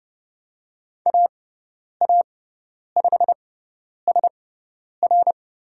Morse code, heard as a keyed tone, about 30 wpm, spaced as for 10 wpm, 700 Hz. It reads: AA5SR
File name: Lesson 16g - Letter c - Call Signs 30wpm